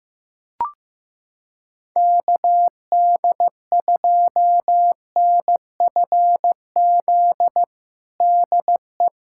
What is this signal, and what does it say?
Morse code, heard as a keyed tone, about 15 wpm, 700 Hz. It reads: KD2NFZ DE
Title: USA Callsign DE Callsign - 6 Characters - 15wpm